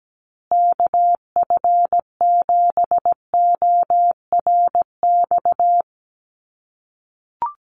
Morse code, heard as a keyed tone, about 17 wpm, 700 Hz. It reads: KF7ORX
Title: USA Call Signs Repeated 3x 17wpm